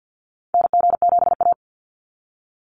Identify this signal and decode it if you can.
Morse code, heard as a keyed tone, about 48 wpm, 700 Hz. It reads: NG7A
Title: NA SOTA Activator Callsigns 50wpm